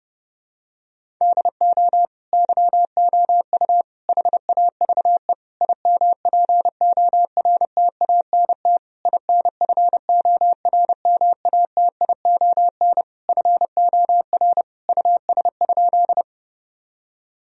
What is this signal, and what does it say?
Morse code, heard as a keyed tone, about 30 wpm, 700 Hz. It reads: DO YOU HAVE IMPORTANT INFORMATION FOR US?
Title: Sentences from Top 1000 Words - Review Part 3 - 30wpm